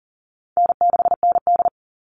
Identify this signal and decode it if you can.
Morse code, heard as a keyed tone, about 40 wpm, 700 Hz. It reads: N6ND
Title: All Cut Numbers - 4-Digit 40wpm